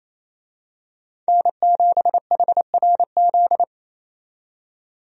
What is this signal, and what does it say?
Morse code, heard as a keyed tone, about 28 wpm, 700 Hz. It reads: N7HRZ
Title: USA Call Signs - 5 Characters - 28wpm